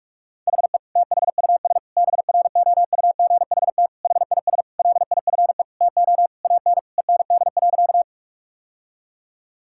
Morse code, heard as a keyed tone, about 41 wpm, 700 Hz. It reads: HE THUS BROUGHT HIS LIFE TO AN END.
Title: Sentences from Top 1000 Words - Review Part 5 - 45wpm